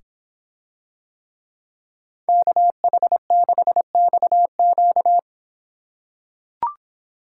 Morse code, heard as a keyed tone, about 25 wpm, 700 Hz. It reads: KH6XQ